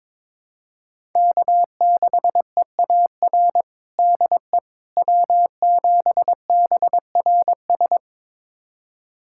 Morse code, heard as a keyed tone, about 22 wpm, 700 Hz. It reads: K6EAR DE W7BRS
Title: USA Callsign DE Callsign - 5 Characters - 22wpm